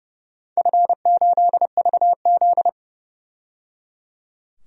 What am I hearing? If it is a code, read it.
Morse code, heard as a keyed tone, about 30 wpm, 700 Hz. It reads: F8VZ
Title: DXCC-100 Call Signs - 4 Characters - 30wpm